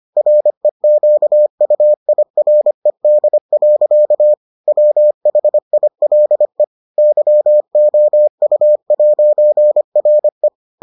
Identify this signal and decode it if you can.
Morse code, heard as a keyed tone, about 25 wpm, 600 Hz. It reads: REQUIRED. WHILE YOU'RE